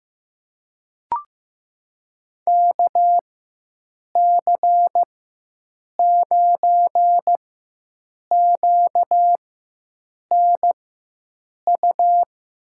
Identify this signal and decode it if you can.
Morse code, heard as a keyed tone, about 15 wpm, spaced as for 7 wpm, 700 Hz. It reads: KC9QNU
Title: USA Call Signs - 4x Character Spacing - 15wpm